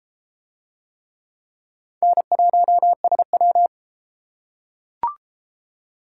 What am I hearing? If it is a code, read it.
Morse code, heard as a keyed tone, about 33 wpm, 700 Hz. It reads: N1SW